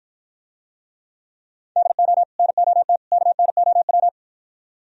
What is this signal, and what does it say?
Morse code, heard as a keyed tone, about 53 wpm, 700 Hz. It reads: DO NOT KNOW